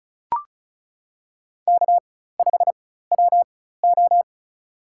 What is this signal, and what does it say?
Morse code, heard as a keyed tone, about 36 wpm, spaced as for 16 wpm, 700 Hz. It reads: K5WO
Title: USA Call Signs - 4 Characters - 4x Character Spacing - 35wpm